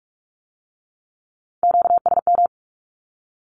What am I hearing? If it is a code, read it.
Morse code, heard as a keyed tone, about 46 wpm, 700 Hz. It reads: QSM
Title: Q-Codes with Spoken Meaning - 45wpm